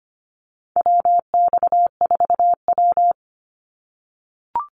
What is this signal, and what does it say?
Morse code, heard as a keyed tone, about 25 wpm, 700 Hz. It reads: WX4W
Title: Call Signs - 25wpm